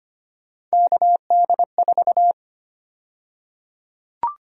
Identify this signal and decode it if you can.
Morse code, heard as a keyed tone, about 25 wpm, 700 Hz. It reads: KD4
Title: USA Call Sign Prefixes - 25wpm